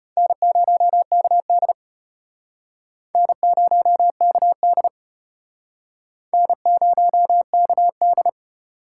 Morse code, heard as a keyed tone, about 34 wpm, 700 Hz. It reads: N0KD N0KD N0KD